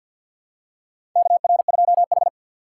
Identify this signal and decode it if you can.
Morse code, heard as a keyed tone, about 52 wpm, 700 Hz. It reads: KR2H
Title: USA Call Signs - 4 Characters - 6x Character Spacing - 50wpm